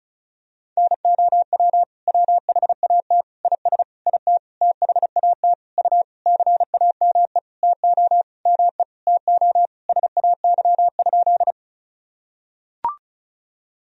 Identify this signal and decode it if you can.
Morse code, heard as a keyed tone, about 34 wpm, 690 Hz. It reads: NOW WHAT IS IT THAT U CAME TO ME TO SAY?
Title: Sentences from Top 200 Words and Abbreviations 35wpm